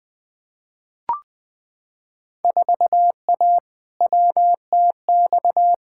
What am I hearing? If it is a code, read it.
Morse code, heard as a keyed tone, about 20 wpm, 700 Hz. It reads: 4A WTX
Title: Field Day Exchanges Encore - 20wpm